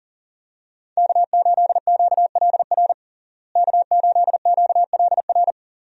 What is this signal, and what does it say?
Morse code, heard as a keyed tone, about 40 wpm, 700 Hz. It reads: K8QLR K8QLR
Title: USA Call Signs - 5 Characters - Repeated 3x - 40wpm